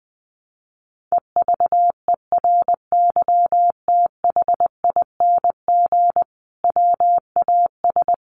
Morse code, heard as a keyed tone, about 20 wpm, 700 Hz. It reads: EVERYTHING WAS
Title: Sentences from Top 700 Words 20wpm